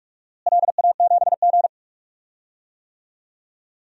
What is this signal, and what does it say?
Morse code, heard as a keyed tone, about 46 wpm, 700 Hz. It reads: LA7G
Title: DXCC-100 Call Signs - 4 Characters - Repeated 2x - 45wpm